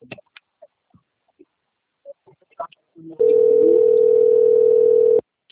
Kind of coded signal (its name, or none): telephone call progress tone